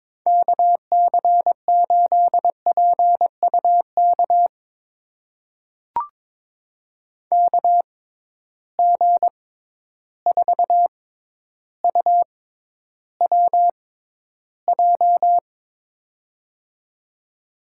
Morse code, heard as a keyed tone, about 22 wpm, spaced as for 11 wpm, 700 Hz. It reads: KC8PUK KG4UWJ
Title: USA Call Signs - 6 Characters - 6x Character Spacing - 22wpm